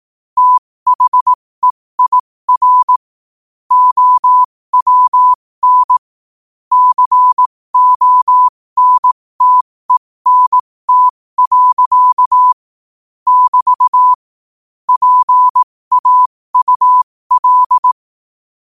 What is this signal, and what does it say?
Morse code, heard as a keyed tone, about 18 wpm, spaced as for 15 wpm, 1000 Hz. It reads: THEIR OWN CONTENT. <BT> PAUL
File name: qotdfiles / 15wpm